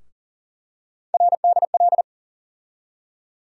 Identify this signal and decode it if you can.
Morse code, heard as a keyed tone, about 40 wpm, 700 Hz. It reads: RDL
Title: USA Call Sign Postfixes - 3 Letters 40wpm